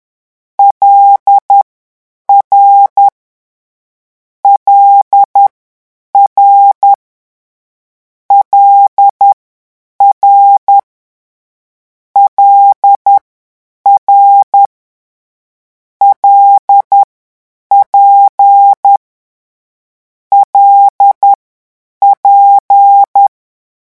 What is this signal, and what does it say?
Morse code, heard as a keyed tone, about 10 wpm, spaced as for 8 wpm, 800 Hz. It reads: LR LR LR LR LP LP
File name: morse mp3 / lecon6-vitesse 10